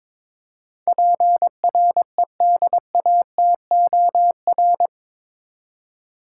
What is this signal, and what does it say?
Morse code, heard as a keyed tone, about 21 wpm, 700 Hz. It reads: PREDATOR
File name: practice / 22wpm / Eight Letter Words - 22wpm